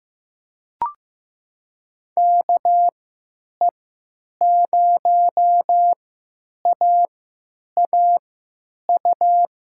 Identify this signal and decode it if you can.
Morse code, heard as a keyed tone, about 15 wpm, spaced as for 9 wpm, 700 Hz. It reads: KE0AAU
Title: USA Call Signs - 6 Characters - 3x Character Spacing - 15wpm